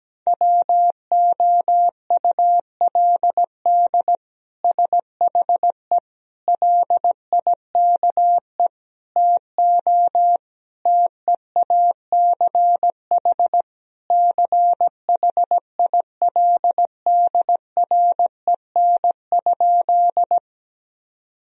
Morse code, heard as a keyed tone, about 17 wpm, 700 Hz. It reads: WOULD SHE LIKE TO TEACH CHILDREN?
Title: Sentences from Top 1000 Words - Review Part 2 - 17wpm